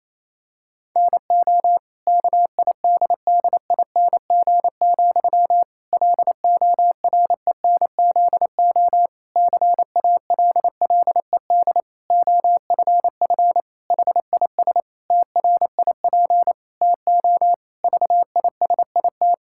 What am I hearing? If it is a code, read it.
Morse code, heard as a keyed tone, about 27 wpm, 700 Hz. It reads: NO KIDDING, LORENZO CALLED OFF HIS TRIP TO VISIT
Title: Pangram Challenge v2 - 28wpm